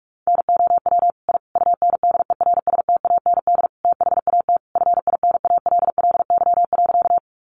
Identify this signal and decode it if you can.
Morse code, heard as a keyed tone, about 44 wpm, 700 Hz. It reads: NOW I UNDERSTAND THAT FINALLY.
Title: Sentences from Top 1000 Words 45wpm